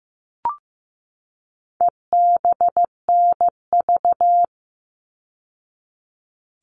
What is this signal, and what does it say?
Morse code, heard as a keyed tone, about 15 wpm, 700 Hz. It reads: EBNV
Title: All Cut Numbers - 4-Digit 15wpm